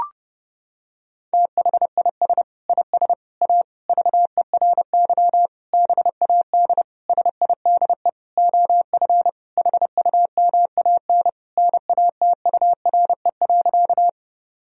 Morse code, heard as a keyed tone, about 30 wpm, 700 Hz. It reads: THIS IS A VERY BAD SIDE OF HUMAN NATURE.
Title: Sentences from Top 1000 Words - Review Part 3 - 30wpm